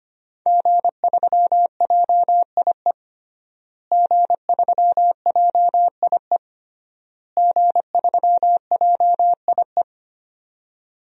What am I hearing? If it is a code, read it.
Morse code, heard as a keyed tone, about 25 wpm, 700 Hz. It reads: G3JIE G3JIE G3JIE
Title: DXCC-100 Call Signs - 5 Characters - Repeated 3x - 25wpm